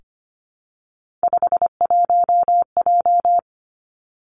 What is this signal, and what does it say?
Morse code, heard as a keyed tone, about 25 wpm, 700 Hz. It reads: 51J
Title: Three Challenging Letter-Number Combinations 25wpm